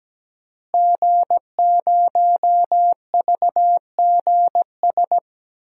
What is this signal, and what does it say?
Morse code, heard as a keyed tone, about 17 wpm, 700 Hz. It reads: G0VGS
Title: DXCC-100 Call Signs - 5 Characters - Repeated 3x - 17wpm